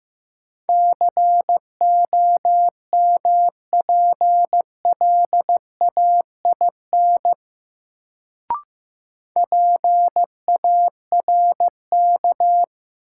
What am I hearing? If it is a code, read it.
Morse code, heard as a keyed tone, about 15 wpm, 700 Hz. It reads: COMPLAIN PARK
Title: Sentences from Top 1300 Words - New Word Drill - 15wpm